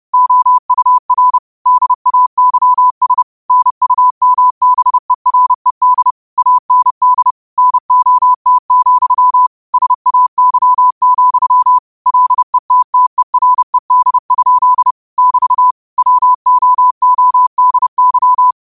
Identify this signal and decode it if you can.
Morse code, heard as a keyed tone, about 30 wpm, 1000 Hz. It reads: OUR DAYS NUMBERED AND NOT, SAY, LETTERED? <BT> WOODY
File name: qotdfiles / 30wpm